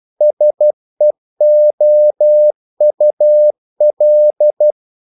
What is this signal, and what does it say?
Morse code, heard as a keyed tone, about 12 wpm, 600 Hz. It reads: SEOUL
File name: CW / Practice / NationalCapitals12